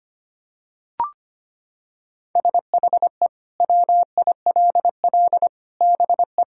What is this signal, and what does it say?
Morse code, heard as a keyed tone, about 25 wpm, 700 Hz. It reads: SHE WILL BE
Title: Sentences from Top 1000 Words - Review Part 2 - 25wpm